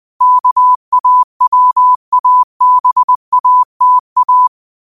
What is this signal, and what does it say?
Morse code, heard as a keyed tone, about 20 wpm, 1000 Hz. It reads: KAWABATA
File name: qotdfiles / 20wpm